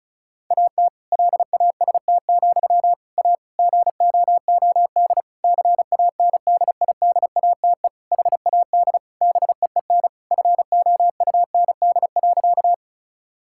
Morse code, heard as a keyed tone, about 36 wpm, 700 Hz. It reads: AT LAST, A GOOD CANDIDATE HAD BEEN FOUND.
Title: Sentences from Top 1600 Words 35wpm